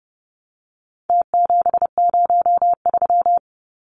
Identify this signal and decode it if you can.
Morse code, heard as a keyed tone, about 30 wpm, 700 Hz. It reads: T703
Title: Numbers - Contest Style - 4-Digit 30wpm